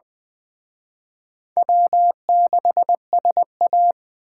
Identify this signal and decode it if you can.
Morse code, heard as a keyed tone, about 20 wpm, 700 Hz. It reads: W6SA
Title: Speed Racing - USA 1x2 Call Signs 17wpm